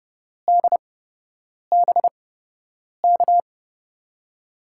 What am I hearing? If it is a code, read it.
Morse code, heard as a keyed tone, about 30 wpm, spaced as for 8 wpm, 700 Hz. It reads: DBK
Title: USA Call Signs - 5 Characters - 8x Character Spacing - 30wpm